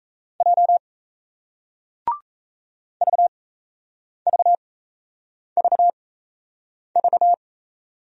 Morse code, heard as a keyed tone, about 37 wpm, spaced as for 8 wpm, 700 Hz. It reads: J VVVV